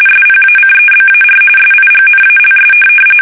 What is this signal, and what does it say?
RTTY, about 45.45 baud. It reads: SPACEBAR SIGNALS